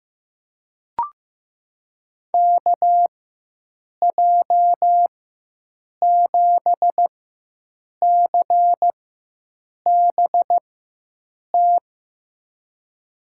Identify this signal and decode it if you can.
Morse code, heard as a keyed tone, about 15 wpm, spaced as for 7 wpm, 700 Hz. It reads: KJ7CBT
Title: USA Call Signs - 4x Character Spacing - 15wpm